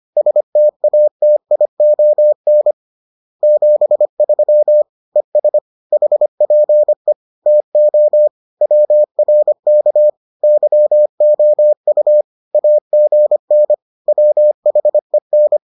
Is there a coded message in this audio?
Yes: Morse code from 0.2 to 15.6 s, heard as a keyed tone, about 25 wpm, 600 Hz. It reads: STATION 73 ES HPE TO WRK YOU AGN WHEN